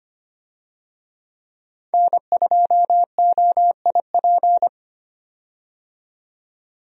Morse code, heard as a keyed tone, about 25 wpm, 700 Hz. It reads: N2OIP
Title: USA Call Signs - 5 Characters - 25wpm